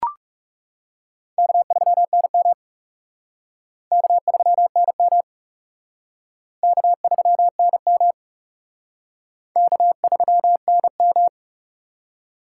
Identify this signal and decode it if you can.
Morse code, heard as a keyed tone, about 37 wpm, 700 Hz. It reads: K3NM K3NM K3NM K3NM
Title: Speed Racing - USA 1x2 Call Signs 30wpm